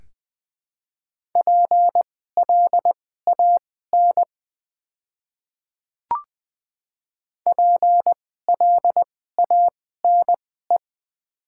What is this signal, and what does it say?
Morse code, heard as a keyed tone, about 20 wpm, spaced as for 15 wpm, 690 Hz. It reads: PLAN PLANE
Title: 2x Character Spacing - Top 500 Words - 20wpm